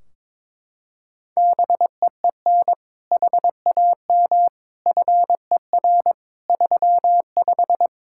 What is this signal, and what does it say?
Morse code, heard as a keyed tone, about 22 wpm, 700 Hz. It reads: BEEN HAM FER 35
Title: Practice QSO - Exchange 3 - More Info - 22wpm